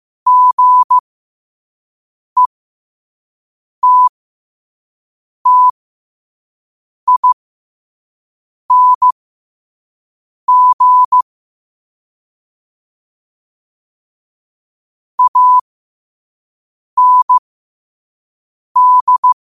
Morse code, heard as a keyed tone, about 15 wpm, spaced as for 5 wpm, 1000 Hz. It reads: GETTING AND
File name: qotdfiles / 5wpm